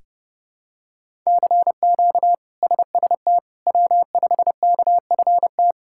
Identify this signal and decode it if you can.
Morse code, heard as a keyed tone, about 29 wpm, 700 Hz. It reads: CQ SST W5KFT
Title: K1USN SST QSO Practice - 30wpm